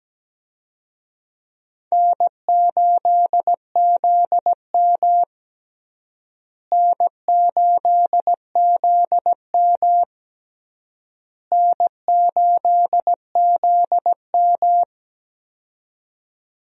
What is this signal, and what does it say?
Morse code, heard as a keyed tone, about 17 wpm, 700 Hz. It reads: N8ZM N8ZM N8ZM
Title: USA Call Signs - 4 Characters - Repeated 3x - 17wpm